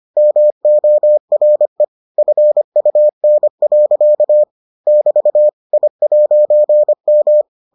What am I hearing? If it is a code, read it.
Morse code, heard as a keyed tone, about 25 wpm, 600 Hz. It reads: MORE FUN. <BT> I'M